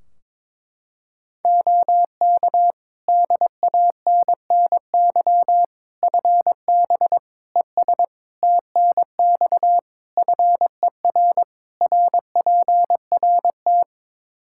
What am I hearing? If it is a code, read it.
Morse code, heard as a keyed tone, about 22 wpm, 700 Hz. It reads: OK DANNY FB ES TNX FER RPRT
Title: Practice QSO - Exchange 2 - Info - 22wpm